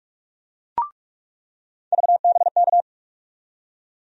Morse code, heard as a keyed tone, about 44 wpm, 700 Hz. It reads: VBK
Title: USA Call Sign Postfixes - 3 Letters 45wpm